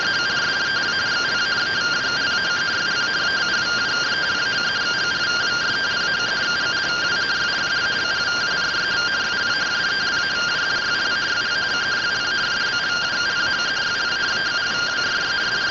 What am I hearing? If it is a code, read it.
Two-tone teleprinter signal, about 170 Hz shift.
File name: findings / AXIAL fr130 3mm 409.gen-39 rtty rx